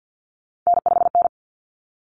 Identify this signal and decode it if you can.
Morse code, heard as a keyed tone, about 50 wpm, 700 Hz. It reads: N5N
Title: Common Cut Numbers - 3-Digit 50wpm